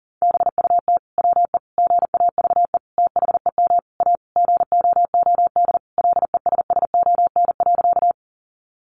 Morse code, heard as a keyed tone, about 40 wpm, 700 Hz. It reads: BUT WE GAVE THEM A GOOD LESSON.
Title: Sentences from Top 1200 Words 40wpm